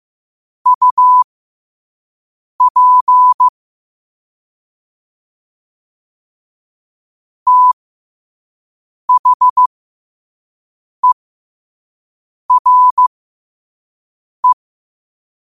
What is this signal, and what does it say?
Morse code, heard as a keyed tone, about 15 wpm, spaced as for 5 wpm, 1000 Hz. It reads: UP THERE
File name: qotdfiles / 5wpm